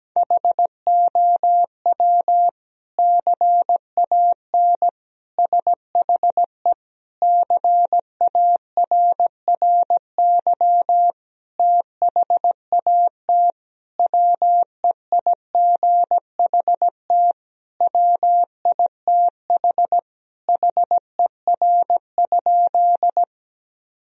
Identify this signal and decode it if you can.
Morse code, heard as a keyed tone, about 17 wpm, 700 Hz. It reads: HOW CAN SHE CARRY THAT WEIGHT WITH HER?
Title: Sentences from Top 1000 Words - Review Part 3 - 17wpm